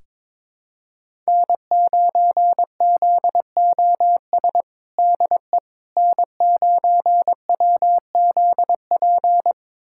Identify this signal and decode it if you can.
Morse code, heard as a keyed tone, about 22 wpm, 700 Hz. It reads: N9ZOS DE N9WZP
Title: USA Callsign DE Callsign - 5 Characters - 22wpm